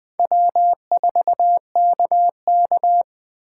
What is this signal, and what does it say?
Morse code, heard as a keyed tone, about 20 wpm, 700 Hz. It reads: W4KK